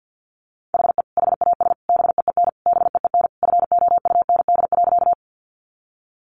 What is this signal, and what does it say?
Morse code, heard as a keyed tone, about 50 wpm, 700 Hz. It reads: HE HAS BEEN BEEN FOUND.
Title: Sentences from Top 300 Words 50wpm